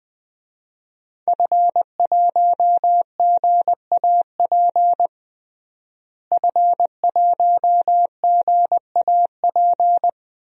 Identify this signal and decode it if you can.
Morse code, heard as a keyed tone, about 21 wpm, 700 Hz. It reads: F1GAP F1GAP
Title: DXCC-100 Call Signs - 5 Characters - Repeated 3x - 20wpm